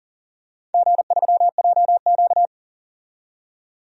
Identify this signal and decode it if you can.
Morse code, heard as a keyed tone, about 41 wpm, 700 Hz. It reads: G3JQ
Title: DXCC-100 Call Signs - 4 Characters - 40wpm